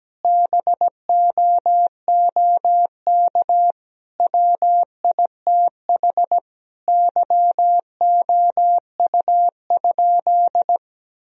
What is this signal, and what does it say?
Morse code, heard as a keyed tone, about 17 wpm, 700 Hz. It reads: BOOK WITH YOU?
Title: Sentences from Top 1000 Words - Review Part 3 - 17wpm